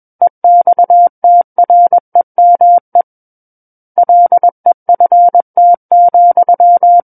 Morse code, heard as a keyed tone, about 21 wpm, 700 Hz. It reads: EXTREME LEFT,
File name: Book One / 12-Chapter 12 - 21 WPM